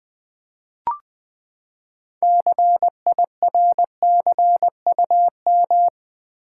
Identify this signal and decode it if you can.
Morse code, heard as a keyed tone, about 20 wpm, 700 Hz. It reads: CIRCUM
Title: Build A Word - 6 Letters At A Time - 12-Letter Words - 20wpm